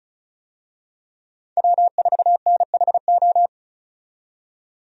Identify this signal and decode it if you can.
Morse code, heard as a keyed tone, about 34 wpm, 700 Hz. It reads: W4NHO
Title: USA Call Signs - 5 Characters - 35wpm